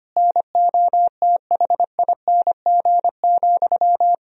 Morse code, heard as a keyed tone, about 25 wpm, 700 Hz. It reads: NOTHING,
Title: Sentences from Top 1000 Words - Review Part 4 - 25wpm